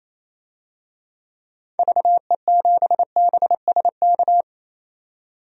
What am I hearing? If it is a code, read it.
Morse code, heard as a keyed tone, about 28 wpm, 700 Hz. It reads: VE7BSK
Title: DXCC-100 Call Signs - 6 Characters - 28wpm